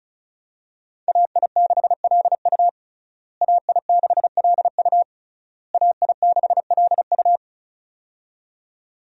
Morse code, heard as a keyed tone, about 36 wpm, 700 Hz. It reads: AI6LU AI6LU AI6LU
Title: USA Call Signs - 5 Characters - Repeated 3x - 35wpm